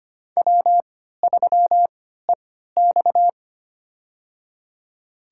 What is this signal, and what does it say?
Morse code, heard as a keyed tone, about 25 wpm, spaced as for 14 wpm, 700 Hz. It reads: W3EX
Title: USA Call Signs - 4 Characters - 3x Character Spacing - 25wpm